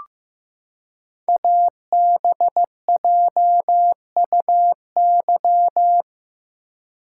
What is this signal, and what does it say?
Morse code, heard as a keyed tone, about 15 wpm, 700 Hz. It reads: ABJUY
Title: Code Groups - 5 Letters - 15wpm